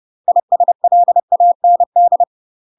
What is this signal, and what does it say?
Morse code, heard as a keyed tone, about 30 wpm, 700 Hz. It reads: ISLAND
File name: UK MASTER QSOs TRAINING / 30WPM / UK MASTER QSOs 0018